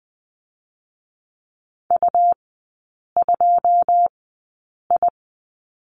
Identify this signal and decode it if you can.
Morse code, heard as a keyed tone, about 20 wpm, spaced as for 8 wpm, 700 Hz. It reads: U2I
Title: 100 Lesson 17e - Letter u - Three Characters 20wpm